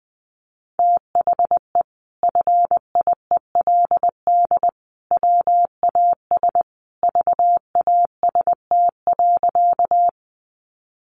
Morse code, heard as a keyed tone, about 20 wpm, 700 Hz. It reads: THE FIELD WAS VAST.
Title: Sentences from Top 1300 Words 20wpm